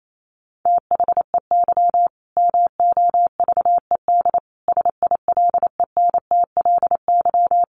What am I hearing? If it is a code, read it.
Morse code, heard as a keyed tone, about 28 wpm, 700 Hz. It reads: THEY MOVED SILENTLY
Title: A Sound Of Thunder 28wpm